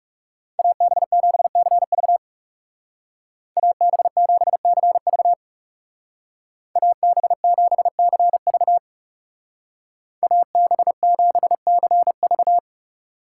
Morse code, heard as a keyed tone, about 39 wpm, 700 Hz. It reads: AB7CV AB7CV AB7CV AB7CV